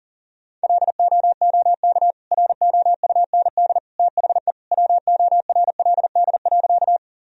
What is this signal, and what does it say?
Morse code, heard as a keyed tone, about 40 wpm, 700 Hz. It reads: LOOK ROUND THE WORLD.
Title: Sentences from Top 1000 Words - Review Part 1 - 40wpm